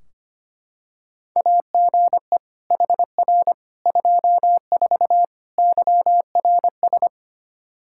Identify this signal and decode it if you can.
Morse code, heard as a keyed tone, about 25 wpm, 700 Hz. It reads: AGE HR 24 YRS
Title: Practice QSO - Exchange 3 - More Info - 25wpm